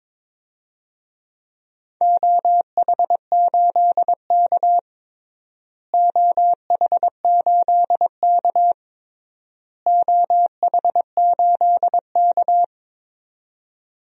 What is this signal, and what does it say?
Morse code, heard as a keyed tone, about 22 wpm, 700 Hz. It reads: OH8K OH8K OH8K
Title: DXCC-100 Call Signs - 4 Characters - Repeated 3x - 22wpm